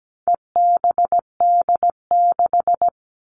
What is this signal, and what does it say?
Morse code, heard as a keyed tone, about 17 wpm, 700 Hz. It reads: EBD6
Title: All Cut Numbers - 4-Digit 17wpm